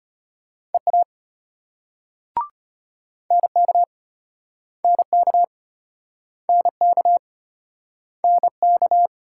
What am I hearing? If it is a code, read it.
Morse code, heard as a keyed tone, about 33 wpm, 700 Hz. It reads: EA NK NK NK NK